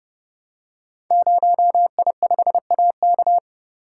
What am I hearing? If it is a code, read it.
Morse code, heard as a keyed tone, about 30 wpm, 700 Hz. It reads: 0I5AK
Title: Code Groups - 5 Letter:Numbers - 30wpm